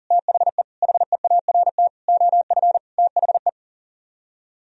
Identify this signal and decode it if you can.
Morse code, heard as a keyed tone, about 51 wpm, 700 Hz. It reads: THE HEART OF THE